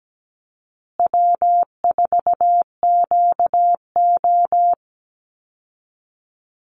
Morse code, heard as a keyed tone, about 17 wpm, 700 Hz. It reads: W4QO
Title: Call Signs Repeated 3x - 17wpm